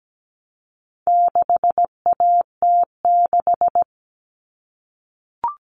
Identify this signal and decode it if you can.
Morse code, heard as a keyed tone, about 17 wpm, 700 Hz. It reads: 6AT6
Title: All Cut Numbers - 4-Digit 17wpm